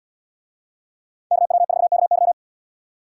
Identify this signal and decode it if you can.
Morse code, heard as a keyed tone, about 77 wpm, 700 Hz. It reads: KW4KY